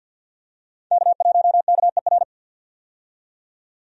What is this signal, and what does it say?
Morse code, heard as a keyed tone, about 50 wpm, 700 Hz. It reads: K1KER
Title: USA Call Signs - 5 Characters - 50wpm